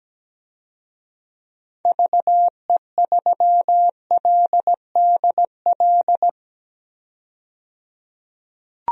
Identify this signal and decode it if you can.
Morse code, heard as a keyed tone, about 17 wpm, 700 Hz. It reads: VE3LDL
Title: DXCC-100 Call Signs - 6 Characters - 17wpm